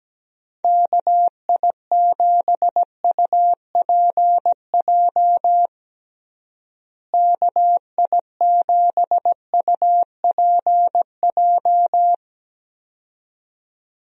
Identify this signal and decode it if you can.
Morse code, heard as a keyed tone, about 17 wpm, 700 Hz. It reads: KI7UPJ KI7UPJ
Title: USA Call Signs - 6 Characters - Repeated 2x - 17wpm